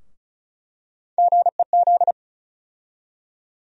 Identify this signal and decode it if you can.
Morse code, heard as a keyed tone, about 36 wpm, 700 Hz. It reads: GEZ
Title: USA Call Sign Postfixes - 3 Letters 35wpm